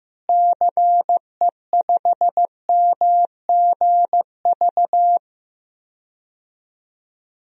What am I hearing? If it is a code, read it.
Morse code, heard as a keyed tone, about 15 wpm, 700 Hz. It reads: CE5MGV
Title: DXCC-100 Call Signs - 6 Characters - Repeated 3x - 15wpm